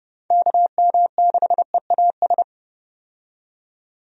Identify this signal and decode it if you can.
Morse code, heard as a keyed tone, about 30 wpm, 700 Hz. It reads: KM6EAS
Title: USA Call Signs - 6 Characters - 30wpm